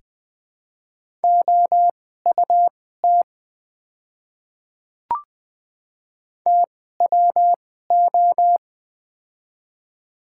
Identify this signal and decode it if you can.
Morse code, heard as a keyed tone, about 20 wpm, spaced as for 14 wpm, 700 Hz. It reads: OUT TWO
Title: 2x Character Spacing - Top 100 Words - 20wpm